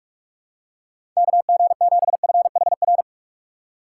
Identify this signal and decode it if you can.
Morse code, heard as a keyed tone, about 43 wpm, 700 Hz. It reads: KG7FHR
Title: USA Call Signs - 6 Characters - 2x Character Spacing - 45wpm